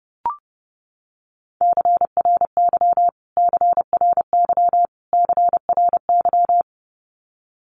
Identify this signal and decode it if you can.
Morse code, heard as a keyed tone, about 30 wpm, 700 Hz. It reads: CRY CRY CRY
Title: Top 500 Words Repeated 30wpm